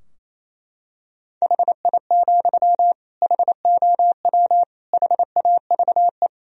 Morse code, heard as a keyed tone, about 28 wpm, 700 Hz. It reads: HI, HOW HAVE
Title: Sentences from Top 1000 Words - Review Part 3 - 28wpm